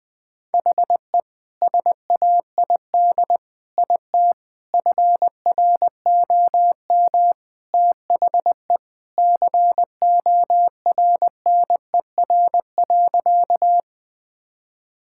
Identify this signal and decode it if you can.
Morse code, heard as a keyed tone, about 20 wpm, 700 Hz. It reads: HE SAID IT FROM THE CORNER.
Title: Sentences from Top 1600 Words 20wpm